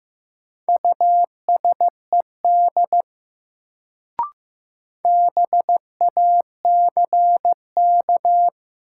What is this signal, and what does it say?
Morse code, heard as a keyed tone, about 15 wpm, 700 Hz. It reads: USED BACK
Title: Sets of 2 Words - New Word Drill - 15wpm